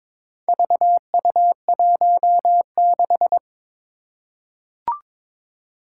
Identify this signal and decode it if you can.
Morse code, heard as a keyed tone, about 22 wpm, 700 Hz. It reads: VU16
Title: Four Challenging Letter-Number Combinations 22wpm